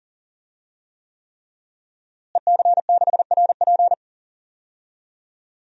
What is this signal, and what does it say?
Morse code, heard as a keyed tone, about 40 wpm, 700 Hz. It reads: EC6RP
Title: DXCC-100 Call Signs - 5 Characters - Repeated 3x - 40wpm